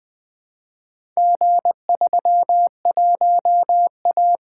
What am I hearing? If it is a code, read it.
Morse code, heard as a keyed tone, about 20 wpm, 700 Hz. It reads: G31A
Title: Four Letter-Number Combinations - 20wpm